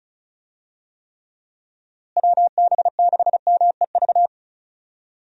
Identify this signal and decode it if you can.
Morse code, heard as a keyed tone, about 34 wpm, 700 Hz. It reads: WB6MEV
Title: USA Call Signs - 6 Characters - Repeated 3x - 35wpm